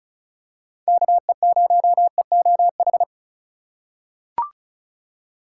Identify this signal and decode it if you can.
Morse code, heard as a keyed tone, about 36 wpm, 700 Hz. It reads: KE0EOH
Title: USA Call Signs - 6 Characters - 3x Character Spacing - 35wpm